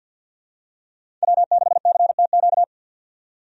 Morse code, heard as a keyed tone, about 48 wpm, 700 Hz. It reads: W6CTQ
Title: USA Call Signs - 5 Characters - 50wpm